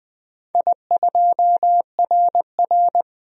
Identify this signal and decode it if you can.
Morse code, heard as a keyed tone, about 20 wpm, 700 Hz. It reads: I2RR
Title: DXCC-100 Call Signs - 4 Characters - 20wpm